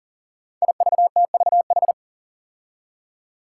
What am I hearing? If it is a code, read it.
Morse code, heard as a keyed tone, about 38 wpm, 700 Hz. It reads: IVTVH
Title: Five Challenging Letter-Number Combinations 40wpm